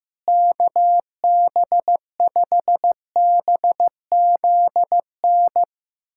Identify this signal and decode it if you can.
Morse code, heard as a keyed tone, about 15 wpm, 700 Hz. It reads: KB5BZN
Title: USA Callsign DE Callsign - 6 Characters - 15wpm